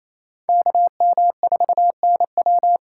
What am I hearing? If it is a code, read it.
Morse code, heard as a keyed tone, about 28 wpm, 700 Hz. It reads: KM4NW
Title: USA Call Signs - 5 Characters - 4x Character Spacing - 28wpm